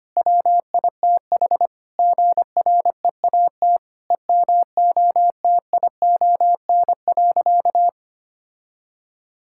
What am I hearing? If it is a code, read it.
Morse code, heard as a keyed tone, about 25 wpm, 700 Hz. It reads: WITH GREAT EMOTION.
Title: Sentences from Top 1600 Words 25wpm